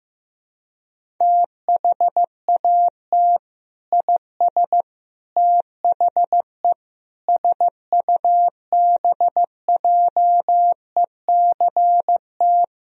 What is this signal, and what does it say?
Morse code, heard as a keyed tone, about 15 wpm, 700 Hz. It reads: THAT IS THE SUBJECT
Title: Sentences from Top 1000 Words - Review Part 4 - 15wpm